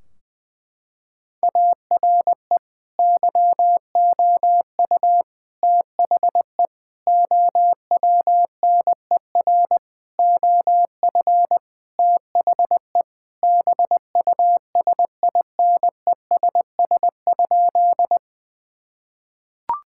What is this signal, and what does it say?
Morse code, heard as a keyed tone, about 20 wpm, 700 Hz. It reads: ARE YOU THE OWNER OF THE BUSINESS?
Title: Sentences from Top 1600 Words 20wpm